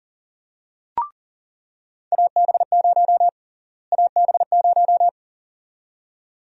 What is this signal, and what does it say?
Morse code, heard as a keyed tone, about 40 wpm, 700 Hz. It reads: AB0 AB0
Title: USA Call Sign Prefixes - Repeated 2x - 40wpm